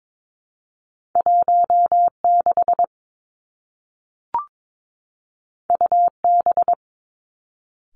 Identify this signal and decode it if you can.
Morse code, heard as a keyed tone, about 22 wpm, 700 Hz. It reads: 1 6 U B
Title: Two Challenging Letter-Number Combinations 22wpm